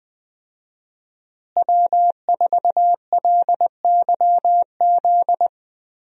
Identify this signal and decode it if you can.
Morse code, heard as a keyed tone, about 21 wpm, 700 Hz. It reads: W4LYZ
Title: USA Call Signs - 5 Characters - Repeated 3x - 20wpm